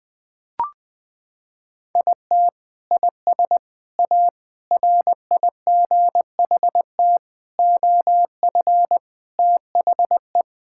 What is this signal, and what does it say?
Morse code, heard as a keyed tone, about 20 wpm, 700 Hz. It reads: IT IS A RIGHT OF THE
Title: Sentences from Top 1000 Words - Review Part 5 - 20wpm